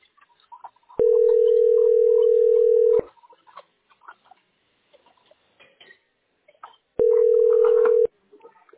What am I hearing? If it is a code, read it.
Ringback tone (North America), 440 + 480 Hz.